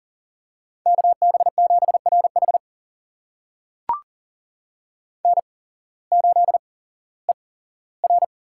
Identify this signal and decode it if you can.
Morse code, heard as a keyed tone, about 40 wpm, spaced as for 19 wpm, 700 Hz. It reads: KB7RH N8ER
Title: USA Call Signs - 8x Character Spacing - 40wpm